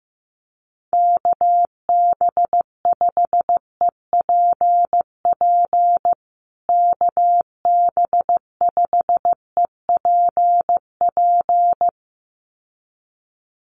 Morse code, heard as a keyed tone, about 15 wpm, 700 Hz. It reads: KB5EPP KB5EPP
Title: USA Call Signs Repeated 2x 15wpm